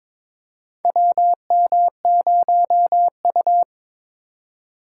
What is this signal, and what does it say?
Morse code, heard as a keyed tone, about 22 wpm, 700 Hz. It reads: WM0U
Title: USA Call Signs - 4 Characters - 3x Character Spacing - 22wpm